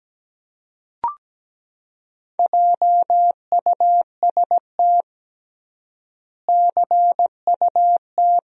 Morse code, heard as a keyed tone, about 17 wpm, 700 Hz. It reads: JUST CUT
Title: 3x Word-Spacing - Sentences from Top 300 Words - 17wpm